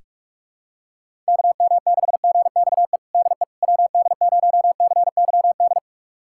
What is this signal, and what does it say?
Morse code, heard as a keyed tone, about 42 wpm, 700 Hz. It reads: KM6GXE DE WD0CYD
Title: USA Callsign DE Callsign - 6 Characters - 45wpm